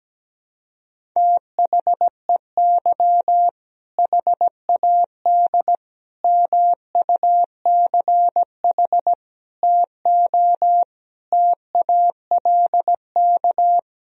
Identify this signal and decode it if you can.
Morse code, heard as a keyed tone, about 17 wpm, 700 Hz. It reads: THEY HAD MUCH TO TALK
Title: Sentences from Top 400 Words and Abbreviations 17wpm